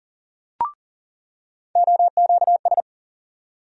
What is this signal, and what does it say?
Morse code, heard as a keyed tone, about 40 wpm, 700 Hz. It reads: OQS
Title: USA Call Sign Postfixes - 3 Letters 40wpm